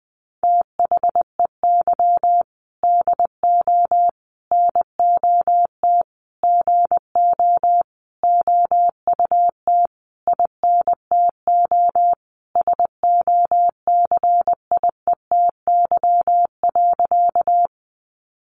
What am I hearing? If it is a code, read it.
Morse code, heard as a keyed tone, about 20 wpm, 700 Hz. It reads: THEY DO NOT GO OUT INTO SOCIETY.
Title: Sentences from Top 700 Words 20wpm